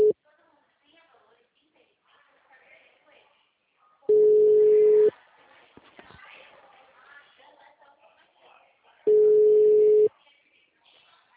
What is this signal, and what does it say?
Ringback tone (Europe, 425 Hz), with speech.
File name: out- -2021